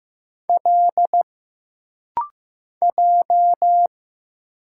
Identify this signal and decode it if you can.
Morse code, heard as a keyed tone, about 15 wpm, 700 Hz. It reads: L J
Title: Single Letters - Rapid-Fire 15wpm